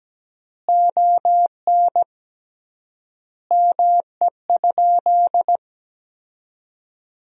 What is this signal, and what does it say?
Morse code, heard as a keyed tone, about 17 wpm, 700 Hz. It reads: ON ME?
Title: 3x Word-Spacing - Sentences from Top 500 Words - 17wpm